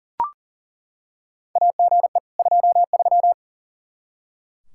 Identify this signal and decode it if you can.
Morse code, heard as a keyed tone, about 40 wpm, 700 Hz. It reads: AGE 23
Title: Practice QSO - Exchange 3 - More Info - 40wpm